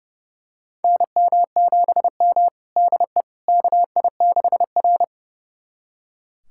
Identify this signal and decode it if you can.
Morse code, heard as a keyed tone, about 30 wpm, 700 Hz. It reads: NM7M DE KI6R
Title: USA Callsign DE Callsign - 4 Characters - 30wpm